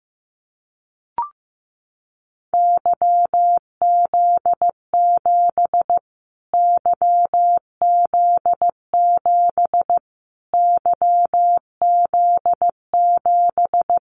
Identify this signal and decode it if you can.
Morse code, heard as a keyed tone, about 15 wpm, 700 Hz. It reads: YZ7 YZ7 YZ7
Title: Callsign Prefixes for Top 25 DXCC - Repeated 3x - 15wpm